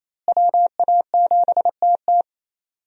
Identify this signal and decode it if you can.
Morse code, heard as a keyed tone, about 28 wpm, 700 Hz. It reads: WA7TT
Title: USA Call Signs - 5 Characters - Repeated 3x - 28wpm